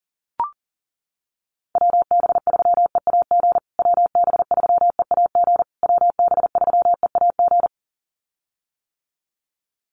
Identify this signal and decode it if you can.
Morse code, heard as a keyed tone, about 40 wpm, 700 Hz. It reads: WB3EAG WB3EAG WB3EAG
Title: USA Call Signs Repeated 3x 40wpm